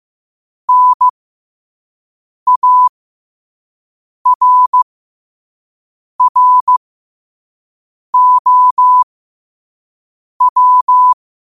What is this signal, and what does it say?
Morse code, heard as a keyed tone, about 15 wpm, spaced as for 5 wpm, 1000 Hz. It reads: NARROW
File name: qotdfiles / 5wpm